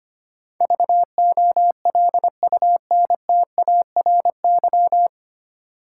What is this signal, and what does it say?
Morse code, heard as a keyed tone, about 25 wpm, 700 Hz. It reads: VOLUNTARY
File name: Nine Letter Words - 25wpm